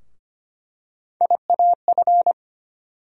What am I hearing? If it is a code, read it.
Morse code, heard as a keyed tone, about 25 wpm, 700 Hz. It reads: IAF
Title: USA Call Sign Postfixes - 3 Letters 25wpm